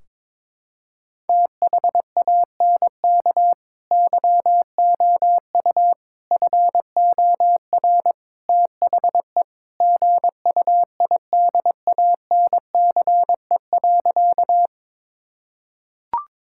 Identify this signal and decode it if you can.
Morse code, heard as a keyed tone, about 22 wpm, 700 Hz. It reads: THANK YOU FOR THE GUIDANCE.
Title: Sentences from Top 1600 Words 22wpm